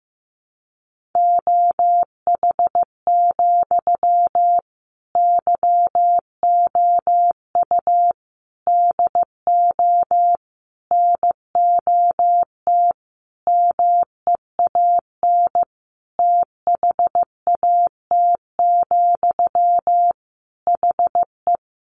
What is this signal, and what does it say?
Morse code, heard as a keyed tone, about 15 wpm, 700 Hz. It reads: OH, YOU DO NOT MEAN THAT, HE
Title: Sentences from Top 600 Words 15wpm